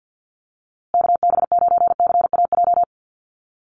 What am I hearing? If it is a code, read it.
Morse code, heard as a keyed tone, about 50 wpm, 700 Hz. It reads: KB9CAJ
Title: USA Call Signs Repeated 2x 50wpm